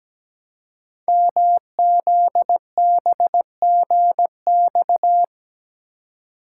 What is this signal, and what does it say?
Morse code, heard as a keyed tone, about 17 wpm, 700 Hz. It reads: MZBGX
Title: Code Groups - 5 Letters - 17wpm